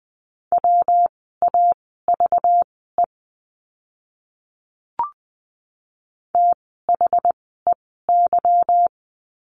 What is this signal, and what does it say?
Morse code, heard as a keyed tone, about 20 wpm, spaced as for 14 wpm, 700 Hz. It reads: WAVE THEY
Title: 192 Lesson 31f - Letter v - Words 20wpm